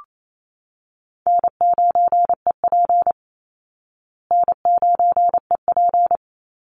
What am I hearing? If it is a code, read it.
Morse code, heard as a keyed tone, about 28 wpm, 700 Hz. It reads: N9EP N9EP
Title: Easy Call Signs Repeated 2x - 28wpm